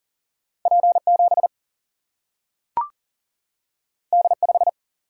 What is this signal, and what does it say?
Morse code, heard as a keyed tone, about 40 wpm, 700 Hz. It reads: P 7 D 5
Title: Single-Letter Single-Number 40wpm